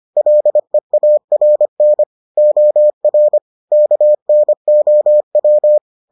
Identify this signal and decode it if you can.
Morse code, heard as a keyed tone, about 25 wpm, 600 Hz. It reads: LEARN OR KNOW